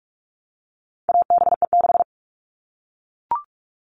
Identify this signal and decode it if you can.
Morse code, heard as a keyed tone, about 44 wpm, 700 Hz. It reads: ABE6
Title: All Cut Numbers - 4-Digit 45wpm